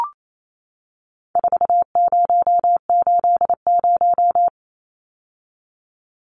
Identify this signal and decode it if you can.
Morse code, heard as a keyed tone, about 28 wpm, 700 Hz. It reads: 4080
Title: Numbers - 4-Digit 28wpm